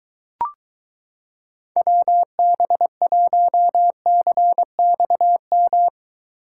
Morse code, heard as a keyed tone, about 23 wpm, 700 Hz. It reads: WB1CXM